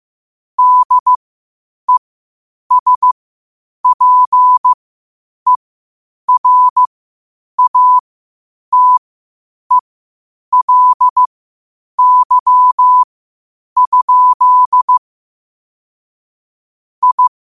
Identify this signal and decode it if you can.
Morse code, heard as a keyed tone, about 15 wpm, spaced as for 8 wpm, 1000 Hz. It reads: DESPERATELY? I
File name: qotdfiles / 8wpm